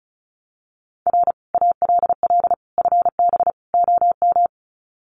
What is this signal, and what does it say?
Morse code, heard as a keyed tone, about 34 wpm, 700 Hz. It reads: R ALL FB OM
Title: Common QSO Phrases 35wpm